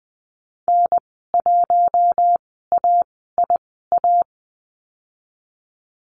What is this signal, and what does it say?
Morse code, heard as a keyed tone, about 20 wpm, spaced as for 15 wpm, 700 Hz. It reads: N1AIA
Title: 042 Lesson 8g - Number 1 - Call Signs 20wpm